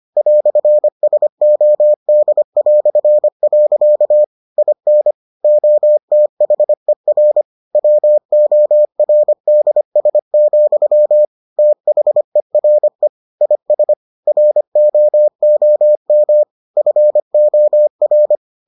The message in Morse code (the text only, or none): "SOD". IN OTHER WORDS, THERE IS ROOM FOR